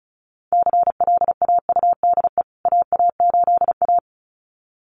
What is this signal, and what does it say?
Morse code, heard as a keyed tone, about 34 wpm, 700 Hz. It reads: CLAUDE AA8A
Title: CWT Exchange 35wpm